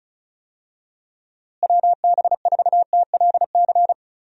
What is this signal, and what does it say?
Morse code, heard as a keyed tone, about 36 wpm, 700 Hz. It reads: WB4TLC
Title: USA Call Signs - 6 Characters - Repeated 2x - 35wpm